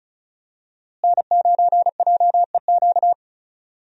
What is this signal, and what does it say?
Morse code, heard as a keyed tone, about 34 wpm, 700 Hz. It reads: N9JEQ
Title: USA Call Signs - 5 Characters - Repeated 3x - 35wpm